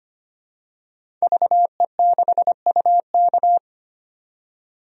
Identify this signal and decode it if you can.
Morse code, heard as a keyed tone, about 25 wpm, 700 Hz. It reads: VE6UK
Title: DXCC-100 Call Signs - 5 Characters - 25wpm